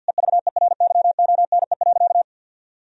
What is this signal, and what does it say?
Morse code, heard as a keyed tone, about 50 wpm, 700 Hz. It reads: EVERYONE.
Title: Sentences from Top 1000 Words - Review Part 3 - 50wpm